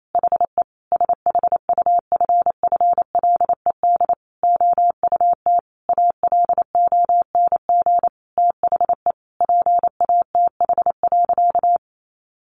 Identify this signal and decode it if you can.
Morse code, heard as a keyed tone, about 28 wpm, 700 Hz. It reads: HE SHUFFLED OUT ALONG THE PATH.
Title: A Sound Of Thunder 28wpm